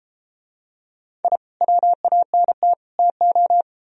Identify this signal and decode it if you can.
Morse code, heard as a keyed tone, about 32 wpm, 700 Hz. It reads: I WANT TO